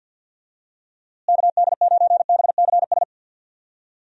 Morse code, heard as a keyed tone, about 49 wpm, 700 Hz. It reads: KD9BCS
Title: USA Call Signs - 6 Characters - Repeated 3x - 50wpm